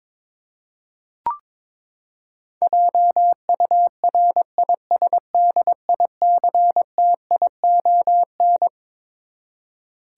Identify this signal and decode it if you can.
Morse code, heard as a keyed tone, about 22 wpm, 700 Hz. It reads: JURISDICTION
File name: Twelve Letter Words - 22wpm